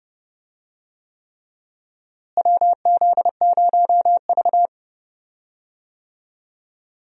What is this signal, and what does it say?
Morse code, heard as a keyed tone, about 30 wpm, 700 Hz. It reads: WZ0V
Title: USA Call Signs - 4 Characters - Repeated 3x - 30wpm